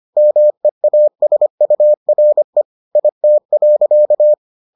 Morse code, heard as a keyed tone, about 25 wpm, 600 Hz. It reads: MEASURE IT.